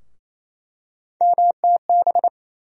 Morse code, heard as a keyed tone, about 28 wpm, 700 Hz. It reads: MTB
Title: USA Call Sign Postfixes - 3 Letters 28wpm